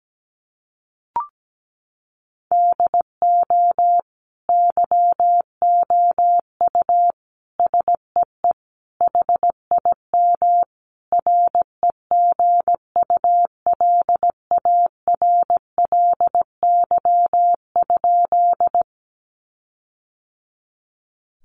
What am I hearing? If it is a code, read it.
Morse code, heard as a keyed tone, about 17 wpm, 700 Hz. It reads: DO YOU SEE HIM REGULARLY?
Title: Sentences from Top 1400 Words 17wpm